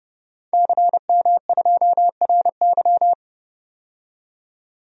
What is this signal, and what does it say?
Morse code, heard as a keyed tone, about 30 wpm, 700 Hz. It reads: CM2RY
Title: DXCC-100 Call Signs - 5 Characters - 30wpm